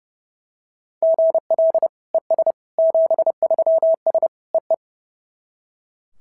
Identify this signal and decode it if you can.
Morse code, heard as a keyed tone, about 27 wpm, 650 Hz. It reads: GL ES 73S EE
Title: SOTA QSO Practice 30wpm